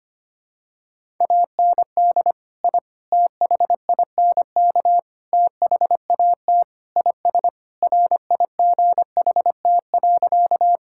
Morse code, heard as a keyed tone, about 25 wpm, 700 Hz. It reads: AND I THINK THAT IS RIGHT.
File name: practice / 25wpm / Sentences with 4-Word Phrases - 25wpm